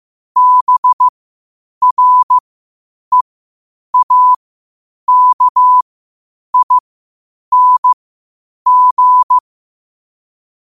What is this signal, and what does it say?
Morse code, heard as a keyed tone, about 15 wpm, spaced as for 8 wpm, 1000 Hz. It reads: BREAKING
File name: qotdfiles / 8wpm